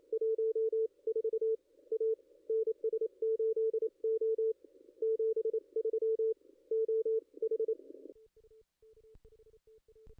Morse code, heard as a keyed tone, about 26 wpm, 440 Hz. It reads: 14 A NS8O 73 OH
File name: contests / ss-cw-k5tr-op-n5rz -2300